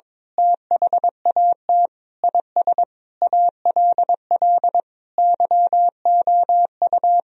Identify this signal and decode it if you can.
Morse code, heard as a keyed tone, about 22 wpm, 700 Hz. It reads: THAT IS ALL YOU
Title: Sentences from Top 1000 Words - Review Part 2 - 22wpm